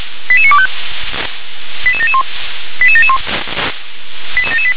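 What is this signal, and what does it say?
Five-tone selective call (ZVEI), heard as quick runs of five tones, tones of about 70 ms. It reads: 88925 88771 88771
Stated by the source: ZVEI five-tone selective call received on SDR, address 88922.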